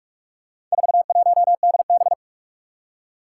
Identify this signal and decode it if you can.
Morse code, heard as a keyed tone, about 44 wpm, 700 Hz. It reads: 41DB
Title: Four Challenging Letter-Number Combinations 45wpm